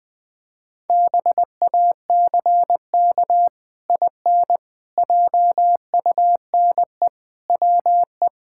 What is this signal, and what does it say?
Morse code, heard as a keyed tone, about 20 wpm, 700 Hz. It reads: BACK IN JUNE WE
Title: Pangram Challenge v2 - 20wpm